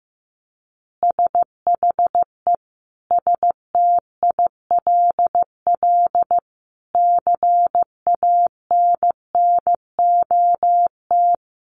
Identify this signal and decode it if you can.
Morse code, heard as a keyed tone, about 15 wpm, 700 Hz. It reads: SHE STILL CANNOT
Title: Additional Easy Phrases 15wpm